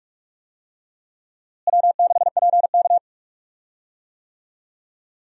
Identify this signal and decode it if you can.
Morse code, heard as a keyed tone, about 46 wpm, 700 Hz. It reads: W6PK
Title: USA Call Signs - 4 Characters - Repeated 2x - 45wpm